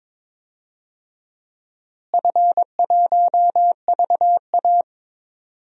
Morse code, heard as a keyed tone, about 22 wpm, 700 Hz. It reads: F1VA
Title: DXCC-100 Call Signs - 4 Characters - Repeated 3x - 22wpm